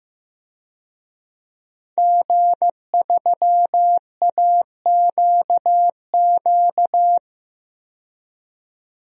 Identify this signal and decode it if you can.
Morse code, heard as a keyed tone, about 15 wpm, 700 Hz. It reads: G3AQQ
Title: DXCC-100 Call Signs - 5 Characters - 15wpm